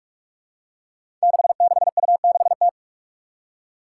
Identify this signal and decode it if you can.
Morse code, heard as a keyed tone, about 44 wpm, 700 Hz. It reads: 66U6T
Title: Five Challenging Letter-Number Combinations 45wpm